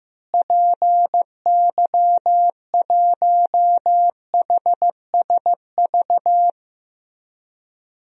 Morse code, heard as a keyed tone, about 15 wpm, 700 Hz. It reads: PY1HSV
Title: DXCC-100 Call Signs - 6 Characters - Repeated 2x - 15wpm